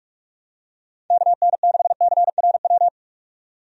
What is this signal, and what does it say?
Morse code, heard as a keyed tone, about 46 wpm, 700 Hz. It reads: KN6CRW
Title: USA Call Signs - 6 Characters - 45wpm